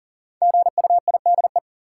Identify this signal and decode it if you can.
Morse code, heard as a keyed tone, about 40 wpm, 700 Hz. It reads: GUIDE
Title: Sentences from Top 600 Words - New Word Drill - 40wpm